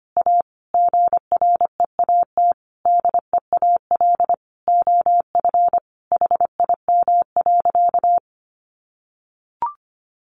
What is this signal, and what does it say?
Morse code, heard as a keyed tone, about 25 wpm, 700 Hz. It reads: A GREAT DEAL OF HIM.
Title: Sentences from Top 800 Words 25wpm